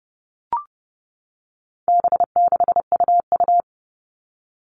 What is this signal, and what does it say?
Morse code, heard as a keyed tone, about 30 wpm, 700 Hz. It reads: B6UU
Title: All Cut Numbers - 4-Digit 30wpm